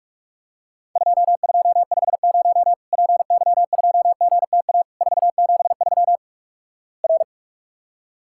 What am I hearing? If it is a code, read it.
Morse code, heard as a keyed tone, about 48 wpm, 700 Hz. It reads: 2250 PY2GTA 473 R
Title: Worked All Europe DX Contest - QTC Exchange - 45wpm